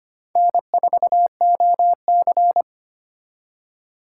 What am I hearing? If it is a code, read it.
Morse code, heard as a keyed tone, about 32 wpm, 700 Hz. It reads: N4OC